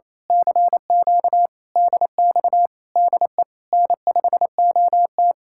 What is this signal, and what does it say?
Morse code, heard as a keyed tone, about 26 wpm, 700 Hz. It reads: CQ DX DE N5OT
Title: Speed-Racing Common QSO Phrases - 28wpm